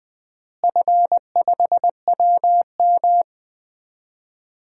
Morse code, heard as a keyed tone, about 20 wpm, 700 Hz. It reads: F5WM
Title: DXCC-100 Call Signs - 4 Characters - Repeated 3x - 20wpm